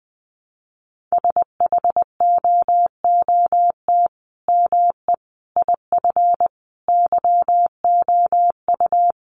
Morse code, heard as a keyed tone, about 20 wpm, 700 Hz. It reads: SHOOT ME IF YOU
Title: Sentences from Top 1200 Words 20wpm